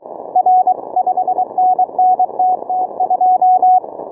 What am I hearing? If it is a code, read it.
Morse code, heard as a keyed tone, about 23 wpm, 740 Hz. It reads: R 5NN TT2
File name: training / morserunner / Example